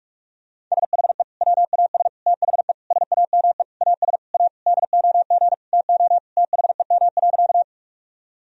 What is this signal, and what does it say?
Morse code, heard as a keyed tone, about 43 wpm, 700 Hz. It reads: SHE WAS THE SAME AS A DOG TO THEM.
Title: Sentences from Top 1000 Words - Review Part 2 - 45wpm